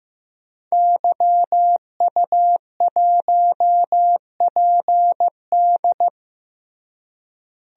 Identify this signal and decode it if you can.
Morse code, heard as a keyed tone, about 15 wpm, 700 Hz. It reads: YU1PD
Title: DXCC-100 Call Signs - 5 Characters - Repeated 2x - 15wpm